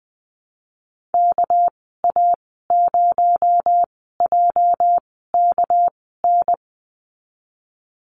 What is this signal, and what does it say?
Morse code, heard as a keyed tone, about 20 wpm, spaced as for 15 wpm, 700 Hz. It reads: KA0JKN
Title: 219 Lesson 35g - Number 0 - Call Signs 20wpm